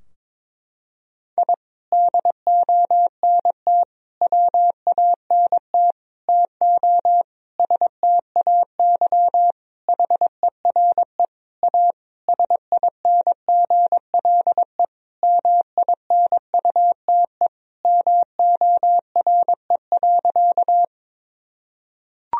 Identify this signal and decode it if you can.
Morse code, heard as a keyed tone, about 21 wpm, 700 Hz. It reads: I DONT WANT TO STAY HERE A SINGLE MINUTE MORE.
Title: Sentences from Top 1000 Words - Review Part 4 - 22wpm